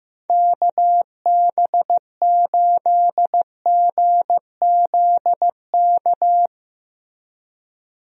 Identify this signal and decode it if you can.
Morse code, heard as a keyed tone, about 15 wpm, 700 Hz. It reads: KB8GZK
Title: USA Call Signs - 6 Characters - Repeated 2x - 15wpm